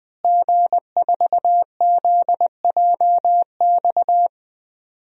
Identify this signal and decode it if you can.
Morse code, heard as a keyed tone, about 21 wpm, 700 Hz. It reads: G4ZJX
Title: DXCC-100 Call Signs - 5 Characters - Repeated 3x - 20wpm